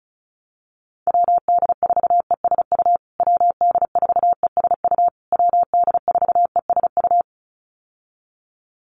Morse code, heard as a keyed tone, about 34 wpm, 700 Hz. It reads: WD4ESU WD4ESU WD4ESU
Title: USA Call Signs Repeated 3x 35wpm